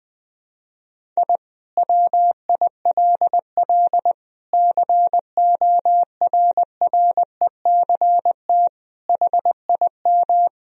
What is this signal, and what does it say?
Morse code, heard as a keyed tone, about 20 wpm, 700 Hz. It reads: I WILL CORRECT HIM
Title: Sentences from Top 1000 Words - Review Part 2 - 20wpm